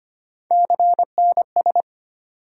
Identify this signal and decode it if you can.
Morse code, heard as a keyed tone, about 25 wpm, 700 Hz. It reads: CNS
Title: USA Call Sign Postfixes - 3 Letters 25wpm